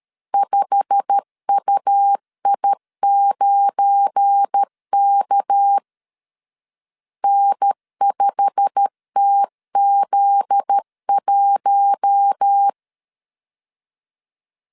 Morse code, heard as a keyed tone, about 13 wpm, 800 Hz. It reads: 5UI9K N5TZ1
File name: CW / DET / 1 9 1500C S A 12wpm 3800